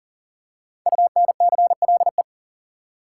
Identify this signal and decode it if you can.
Morse code, heard as a keyed tone, about 40 wpm, 700 Hz. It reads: UNCLE
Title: Five Letter Words - 40wpm